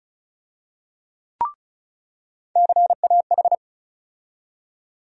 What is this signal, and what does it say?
Morse code, heard as a keyed tone, about 36 wpm, 700 Hz. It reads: CAH
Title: USA Call Sign Postfixes - 3 Letters 35wpm